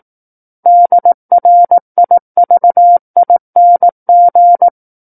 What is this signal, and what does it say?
Morse code, heard as a keyed tone, about 18 wpm, 700 Hz. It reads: DRIVING
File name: 18 WPM